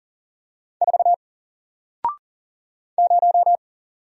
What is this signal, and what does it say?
Morse code, heard as a keyed tone, about 40 wpm, 700 Hz. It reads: 4 0
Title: Single Character (Letter:Number:Punctuation:Prosign) Alt 40wpm